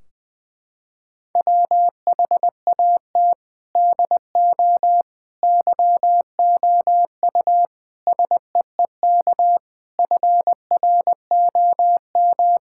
Morse code, heard as a keyed tone, about 20 wpm, 700 Hz. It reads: WHAT DO YOU SEEK FROM
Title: Sentences from Top 1600 Words 20wpm